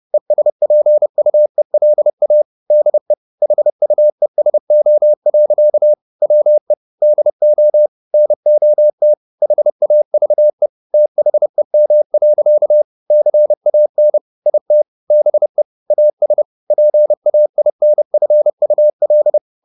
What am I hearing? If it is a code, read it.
Morse code, heard as a keyed tone, about 30 wpm, 600 Hz. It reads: ESPUELA DE HUESO. WE DO NOT HAVE THEM. CAN IT BE AS PAINFUL
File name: the old man and the sea / the old man and the sea 1 30wpm